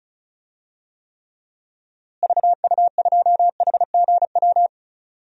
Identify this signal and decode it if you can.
Morse code, heard as a keyed tone, about 36 wpm, 700 Hz. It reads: VU2HGW
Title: DXCC-100 Call Signs - 6 Characters - Repeated 2x - 35wpm